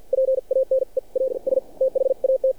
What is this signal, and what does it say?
Morse code, heard as a keyed tone, about 49 wpm, spaced as for 40 wpm, 530 Hz. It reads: PANE AS THAT
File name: archived / CRSnet